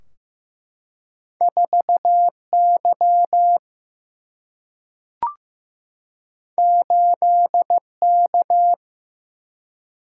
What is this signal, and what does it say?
Morse code, heard as a keyed tone, about 15 wpm, 700 Hz. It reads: 4 Y 8 K
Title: Single-Letter Single-Number 15wpm